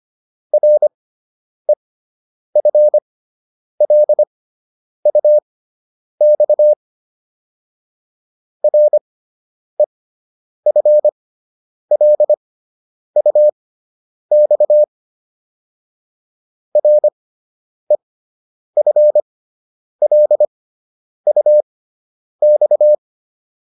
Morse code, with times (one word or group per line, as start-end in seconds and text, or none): 0.5-6.7 REFLUX
8.6-14.8 REFLUX
16.8-22.9 REFLUX